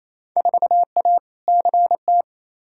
Morse code, heard as a keyed tone, about 28 wpm, 700 Hz. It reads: 4A CT
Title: Field Day Exchanges Encore - 28wpm